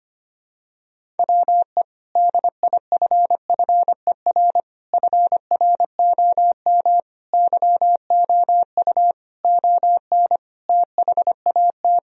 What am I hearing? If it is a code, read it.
Morse code, heard as a keyed tone, about 25 wpm, 700 Hz. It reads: WE DIFFER FROM YOU ON THAT
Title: Sentences from Top 1000 Words - Review Part 1 - 25wpm